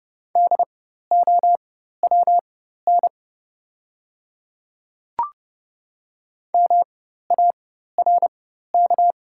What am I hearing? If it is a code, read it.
Morse code, heard as a keyed tone, about 29 wpm, spaced as for 14 wpm, 690 Hz. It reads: DOWN MARK
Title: 4x Character Spacing - Top 300 Words - 30wpm